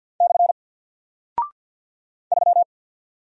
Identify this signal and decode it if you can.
Morse code, heard as a keyed tone, about 48 wpm, 700 Hz. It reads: / 3
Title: Single Character (Letter:Number:Punctuation:Prosign) Alt 50wpm